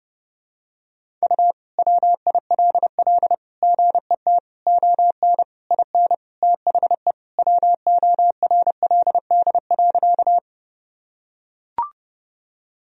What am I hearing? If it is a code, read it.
Morse code, heard as a keyed tone, about 30 wpm, 700 Hz. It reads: U WILL GET ON IN THE WORLD.
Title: Sentences from Top 200 Words and Abbreviations 30wpm